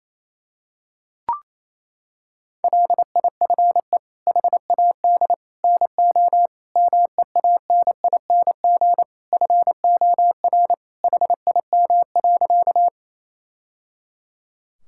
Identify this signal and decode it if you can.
Morse code, heard as a keyed tone, about 28 wpm, 700 Hz. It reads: LIFE HAD NO MEANING FOR HIM.
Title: Sentences from Top 1000 Words - Review Part 6 - 28wpm